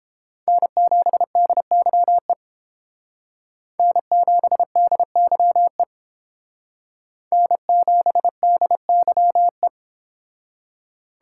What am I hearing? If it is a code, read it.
Morse code, heard as a keyed tone, about 29 wpm, 700 Hz. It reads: N7DYE N7DYE N7DYE